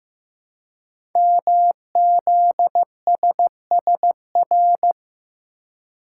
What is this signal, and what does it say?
Morse code, heard as a keyed tone, about 15 wpm, 700 Hz. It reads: MZSSR
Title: Code Groups - 5 Letters - 15wpm